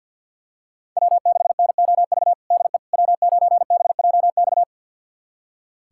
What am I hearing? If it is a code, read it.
Morse code, heard as a keyed tone, about 47 wpm, 700 Hz. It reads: W6NOV DE W9BJX
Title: USA Callsign DE Callsign - 5 Characters - 50wpm